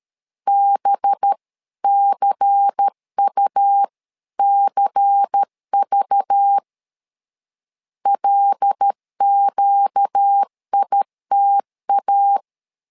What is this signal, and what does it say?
Morse code, heard as a keyed tone, about 13 wpm, 800 Hz. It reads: BCUCV LQITA
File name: LEZIONI / 5 LBVC / 5 LBVC 12wpm 1955